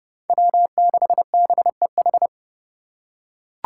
Morse code, heard as a keyed tone, about 29 wpm, 700 Hz. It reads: W6BEH